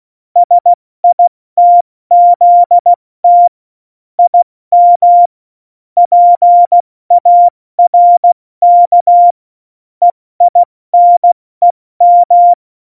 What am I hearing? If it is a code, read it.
Morse code, heard as a keyed tone, about 16 wpm, 700 Hz. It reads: SITZT IM PARK EINEM
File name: Morsekurs / Witze / Witz 04. Juni 2019, 70 F80